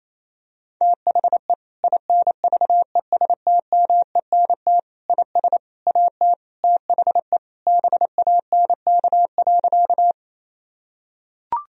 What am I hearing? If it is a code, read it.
Morse code, heard as a keyed tone, about 28 wpm, 700 Hz. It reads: THE INVESTMENT IS AT THE BANK.
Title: Sentences from Top 1000 Words - Review Part 4 - 28wpm